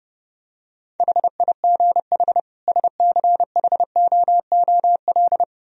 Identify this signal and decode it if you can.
Morse code, heard as a keyed tone, about 30 wpm, 700 Hz. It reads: HIGH SCHOOL